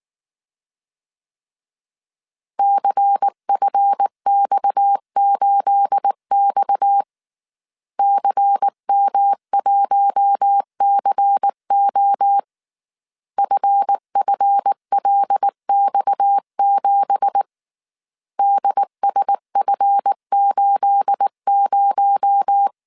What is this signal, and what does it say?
Morse code, heard as a keyed tone, about 19 wpm, 800 Hz. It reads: CFX8X CM1CO FFLX7 DSF80
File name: CW / DET / 1 9 1500C S A 18wpm 2459